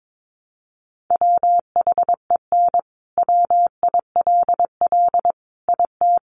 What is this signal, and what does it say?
Morse code, heard as a keyed tone, about 22 wpm, 700 Hz. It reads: WHEN WILL IT
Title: Sentences from Top 200 Words 22wpm